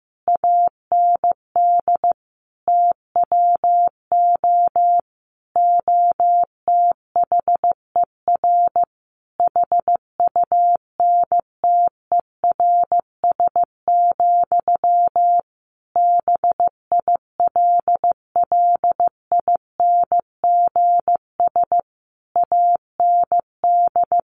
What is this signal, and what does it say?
Morse code, heard as a keyed tone, about 15 wpm, 700 Hz. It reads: AND TWO OTHER HUNTERS, BILLINGS AND
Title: A Sound Of Thunder 15wpm